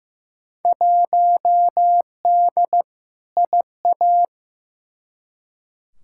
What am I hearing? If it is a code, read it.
Morse code, heard as a keyed tone, about 15 wpm, 700 Hz. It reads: 1D IA
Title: Field Day Exchanges - 15wpm